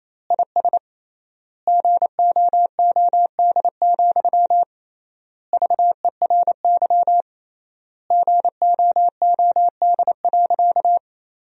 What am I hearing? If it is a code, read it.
Morse code, heard as a keyed tone, about 28 wpm, 700 Hz. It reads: IS GOOD, VERY GOOD.
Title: 3x Word-Spacing - Sentences from Top 200 Words - 28wpm